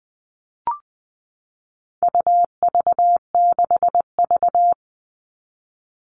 Morse code, heard as a keyed tone, about 20 wpm, 700 Hz. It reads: UV6V
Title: All Cut Numbers - 4-Digit 20wpm